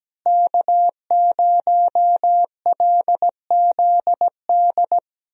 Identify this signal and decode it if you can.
Morse code, heard as a keyed tone, about 17 wpm, 700 Hz. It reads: K0LZD
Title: USA Call Signs - 5 Characters - Repeated 3x - 17wpm